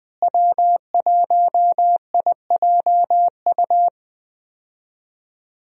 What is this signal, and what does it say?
Morse code, heard as a keyed tone, about 19 wpm, 700 Hz. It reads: W1IJU
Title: USA Call Signs - 5 Characters - Repeated 3x - 20wpm